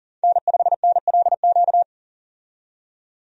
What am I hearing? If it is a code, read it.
Morse code, heard as a keyed tone, about 37 wpm, 700 Hz. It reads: N5NLQ
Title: USA Call Signs - 5 Characters - 40wpm